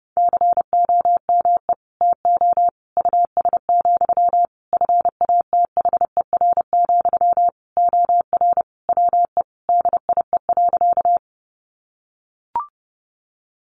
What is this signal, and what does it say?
Morse code, heard as a keyed tone, about 28 wpm, 700 Hz. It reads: COME TO US, FATHER, OR WE DIE.
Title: Sentences from Top 1300 Words 30wpm